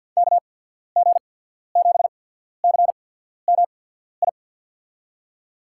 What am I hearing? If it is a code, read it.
Morse code, heard as a keyed tone, about 54 wpm, spaced as for 14 wpm, 700 Hz. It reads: KG7CMI
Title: USA Call Signs - 6 Characters - 8x Character Spacing - 50wpm